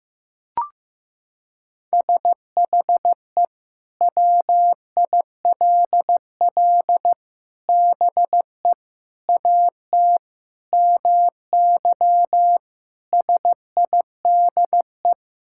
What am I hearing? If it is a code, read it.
Morse code, heard as a keyed tone, about 15 wpm, 700 Hz. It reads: SHE WILL BE AT MY SIDE
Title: Sentences from Top 1600 Words 15wpm